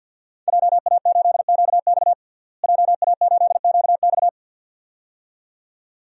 Morse code, heard as a keyed tone, about 49 wpm, 700 Hz. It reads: JA8QX JA8QX
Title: DXCC-100 Call Signs - 5 Characters - Repeated 2x - 50wpm